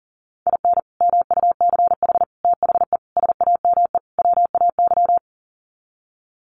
Morse code, heard as a keyed tone, about 40 wpm, 700 Hz. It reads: IN MUCH THE SAME WAY
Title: Sets of 5 Words Encore 40wpm